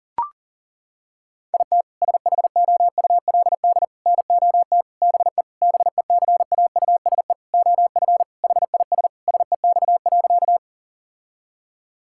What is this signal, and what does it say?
Morse code, heard as a keyed tone, about 37 wpm, 700 Hz. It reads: IT SHOULD NOT BE BECAUSE OF HIS SEX.
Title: Sentences from Top 1600 Words 40wpm